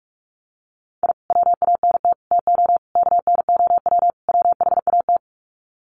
Morse code, heard as a keyed tone, about 44 wpm, 690 Hz. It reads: I WANT TO KNOW WHAT
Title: Sets of 5 Words Encore2 45wpm